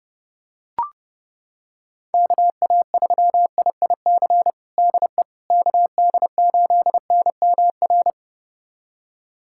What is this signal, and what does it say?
Morse code, heard as a keyed tone, about 30 wpm, 700 Hz. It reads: KA3IIC DE KD8NMR
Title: USA Callsign DE Callsign - 6 Characters - 30wpm